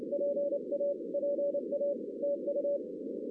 Morse code, heard as a keyed tone, about 28 wpm, 570 Hz, with noise. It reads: PA PA TU
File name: 2016 arrl dx cw / 2016-ARRL-DX-CW-K5TR- - 0-9